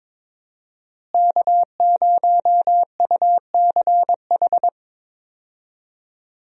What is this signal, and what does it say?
Morse code, heard as a keyed tone, about 22 wpm, 700 Hz. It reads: K0UCH
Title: USA Call Signs - 5 Characters - Repeated 3x - 22wpm